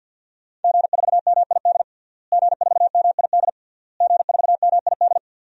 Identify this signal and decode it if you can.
Morse code, heard as a keyed tone, about 50 wpm, 700 Hz. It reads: G4MID G4MID G4MID
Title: DXCC-100 Call Signs - 5 Characters - Repeated 3x - 50wpm